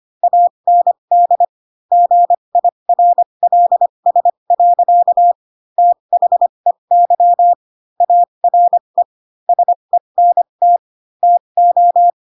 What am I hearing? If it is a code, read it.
Morse code, heard as a keyed tone, about 25 wpm, spaced as for 22 wpm, 700 Hz. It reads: AND GIRLS. THEY ARE SENT TO
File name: Pinocchio / Pinocchio ch4 18 25 ik1hge